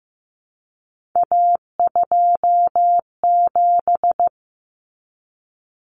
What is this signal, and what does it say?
Morse code, heard as a keyed tone, about 15 wpm, 700 Hz. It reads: A27
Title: Common Cut Numbers - 3-Digit 15wpm